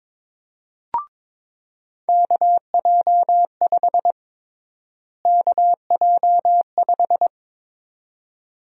Morse code, heard as a keyed tone, about 22 wpm, 700 Hz. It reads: KJ5 KJ5
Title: USA Call Sign Prefixes - Repeated 2x - 22wpm